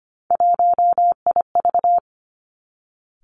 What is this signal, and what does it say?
Morse code, heard as a keyed tone, about 25 wpm, 700 Hz. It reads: 1IV
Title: Three Challenging Letter-Number Combinations 25wpm